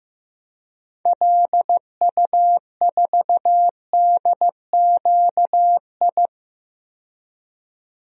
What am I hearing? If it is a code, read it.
Morse code, heard as a keyed tone, about 15 wpm, 700 Hz. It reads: LU4DQI
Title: DXCC-100 Call Signs - 6 Characters - 15wpm